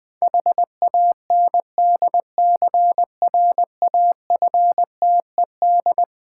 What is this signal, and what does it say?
Morse code, heard as a keyed tone, about 20 wpm, 700 Hz. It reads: HANDCRAFTED
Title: Difficult Words with Hints - 20wpm